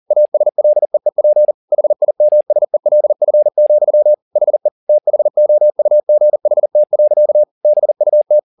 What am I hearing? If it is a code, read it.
Morse code, heard as a keyed tone, about 40 wpm, 600 Hz. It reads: ASLEEP HIMSELF, HE THOUGHT. BUT
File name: the old man and the sea 2 40wpm